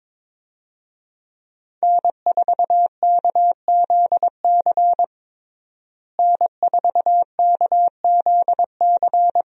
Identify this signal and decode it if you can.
Morse code, heard as a keyed tone, about 22 wpm, 700 Hz. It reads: N4KZC N4KZC
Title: USA Call Signs - 5 Characters - Repeated 3x - 22wpm